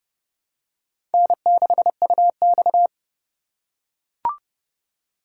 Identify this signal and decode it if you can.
Morse code, heard as a keyed tone, about 30 wpm, 700 Hz. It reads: N6UX
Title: USA Call Signs - 4 Characters - 8x Character Spacing - 30wpm